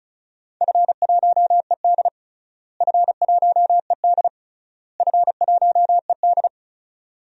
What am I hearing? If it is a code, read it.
Morse code, heard as a keyed tone, about 34 wpm, 700 Hz. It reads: F1ED F1ED F1ED
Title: DXCC-100 Call Signs - 4 Characters - Repeated 3x - 35wpm